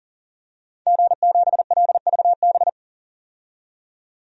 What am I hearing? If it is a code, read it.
Morse code, heard as a keyed tone, about 40 wpm, 700 Hz. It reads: G7LVB
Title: DXCC-100 Call Signs - 5 Characters - 40wpm